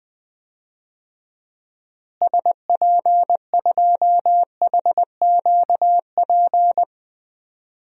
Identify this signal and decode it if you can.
Morse code, heard as a keyed tone, about 20 wpm, 700 Hz. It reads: SP2HQP
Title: DXCC-100 Call Signs - 6 Characters - Repeated 3x - 20wpm